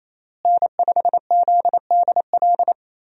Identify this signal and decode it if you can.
Morse code, heard as a keyed tone, about 28 wpm, 700 Hz. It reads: N5ZDL
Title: USA Call Signs - 5 Characters - Repeated 3x - 28wpm